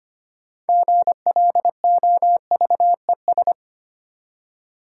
Morse code, heard as a keyed tone, about 25 wpm, 690 Hz. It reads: GLOVES
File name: practice / 25wpm / Easy Words with Hints - 25wpm